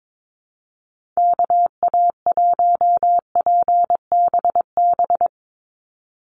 Morse code, heard as a keyed tone, about 22 wpm, 700 Hz. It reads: KA1PBB
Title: USA Call Signs 22wpm